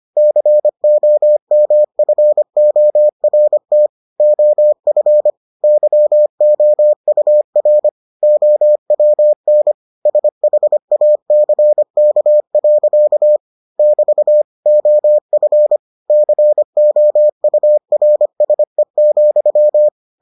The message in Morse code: COMFORT OF YOUR OWN SHACK. <BT> OF COURSE,

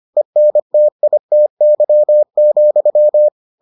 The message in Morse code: ENTITY,

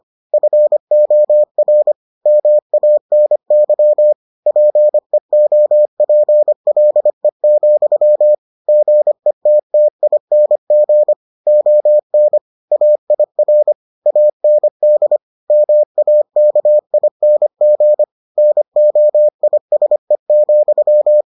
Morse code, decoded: FOR MANY PEOPLE, GETTING ON AIR AND MAKING NOISE,

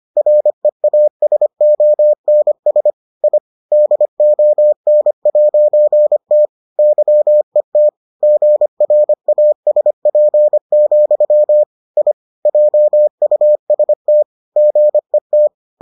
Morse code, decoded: REASONS I DON'T YET GRASP, I JUST GET

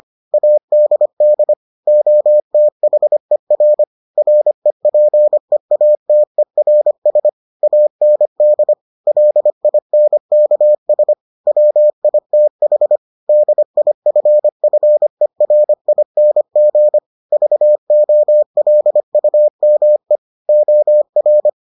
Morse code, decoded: ADD OTHER REPEATERS AND LINKS WITH DIFFERING VOLUME OR

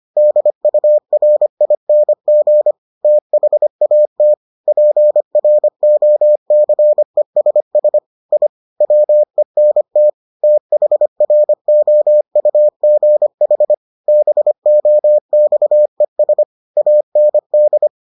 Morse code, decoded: DURING THAT PROCESS I WENT THROUGH BOXES AND

